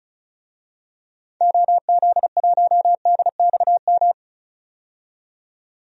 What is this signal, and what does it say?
Morse code, heard as a keyed tone, about 35 wpm, 700 Hz. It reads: OZ1DXM
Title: DXCC-100 Call Signs - 6 Characters - 35wpm